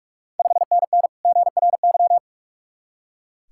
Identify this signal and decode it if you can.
Morse code, heard as a keyed tone, about 45 wpm, 690 Hz. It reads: 5NN GRY
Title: Ontario QSO Party Exchange 45wpm